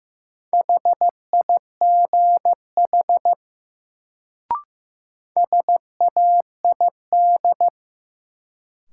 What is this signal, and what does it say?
Morse code, heard as a keyed tone, about 15 wpm, 700 Hz. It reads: HIGH SAID
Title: Sets of 2 Words - New Word Drill - 15wpm